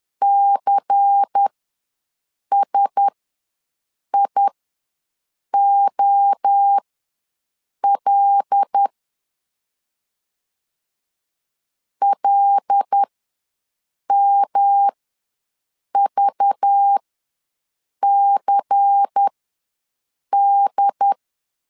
Morse code, heard as a keyed tone, about 10 wpm, spaced as for 6 wpm, 800 Hz. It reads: CSIOL LMVCD
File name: LEZIONI / 5 LBVC / 5 LBVC ws 10wpm 3733